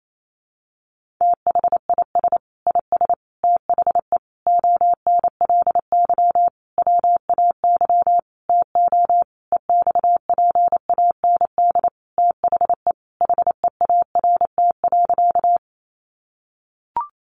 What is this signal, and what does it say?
Morse code, heard as a keyed tone, about 28 wpm, 700 Hz. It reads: THIS IS THE ONLY WAY TO EXPAND THE HEART.
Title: Sentences from Top 1400 Words 28wpm